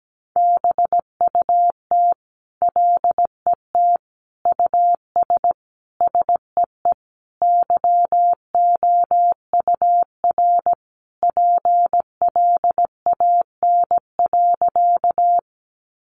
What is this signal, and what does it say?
Morse code, heard as a keyed tone, about 17 wpm, 700 Hz. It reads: BUT LET US SEE YOUR PLAN.
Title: Sentences from Top 500 Words 17wpm